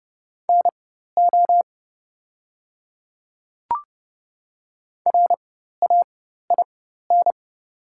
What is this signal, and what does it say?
Morse code, heard as a keyed tone, about 30 wpm, spaced as for 14 wpm, 700 Hz. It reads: NO RAIN
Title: 4x Character Spacing - Top 300 Words - 30wpm